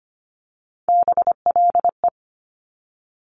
Morse code, heard as a keyed tone, about 25 wpm, 700 Hz. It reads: BLE
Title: Top Three Letter Combinations in Words 25wpm